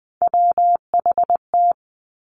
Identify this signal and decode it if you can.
Morse code, heard as a keyed tone, about 20 wpm, 700 Hz. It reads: WHT
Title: Three Challenging Letter-Number Combinations 20wpm